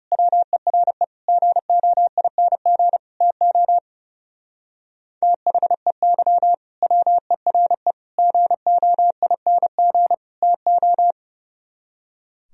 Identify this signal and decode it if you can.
Morse code, heard as a keyed tone, about 32 wpm, 700 Hz. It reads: WERE GOING TO THEY WERE GOING TO
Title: Speed-Racing - Sets of 4 Words 30wpm